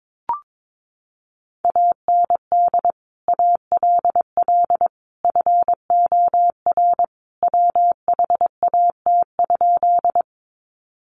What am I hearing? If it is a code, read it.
Morse code, heard as a keyed tone, about 22 wpm, 700 Hz. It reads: AND ALL FOR WHAT?
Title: Sentences from Top 100 Words 22wpm